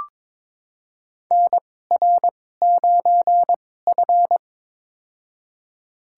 Morse code, heard as a keyed tone, about 22 wpm, spaced as for 16 wpm, 700 Hz. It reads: NR9F
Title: USA Call Signs - 4 Characters - 2x Character Spacing - 22wpm